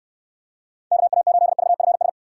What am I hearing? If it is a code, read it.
Morse code, heard as a keyed tone, about 69 wpm, 700 Hz. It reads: KA8VFS